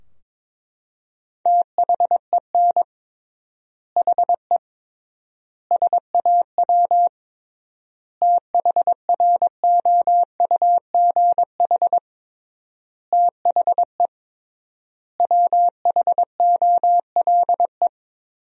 Morse code, heard as a keyed tone, about 22 wpm, 700 Hz. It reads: THEN HE SAW THROUGH THE WHOLE
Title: 3x Word-Spacing - Sentences from Top 400 Words - 22wpm